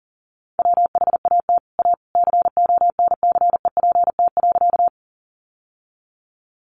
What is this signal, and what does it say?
Morse code, heard as a keyed tone, about 40 wpm, 700 Hz. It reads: WHAT A CONCEPT.
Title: Sentences from Top 1100 Words 40wpm